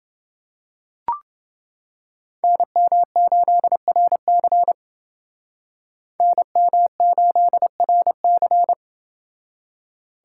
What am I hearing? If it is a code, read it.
Morse code, heard as a keyed tone, about 28 wpm, 700 Hz. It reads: NM8RC NM8RC